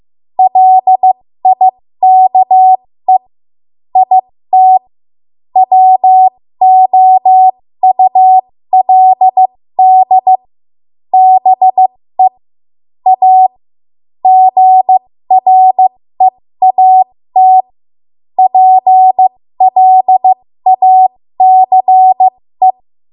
Morse code, heard as a keyed tone, about 15 wpm, spaced as for 13 wpm, 750 Hz. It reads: LIKE IT WOULD BE A GREAT PLACE
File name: Archive / 13 WPM / 13WPM